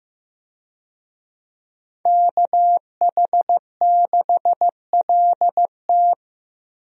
Morse code, heard as a keyed tone, about 16 wpm, 700 Hz. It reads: KH6LT
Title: USA Call Signs - 5 Characters - Repeated 2x - 15wpm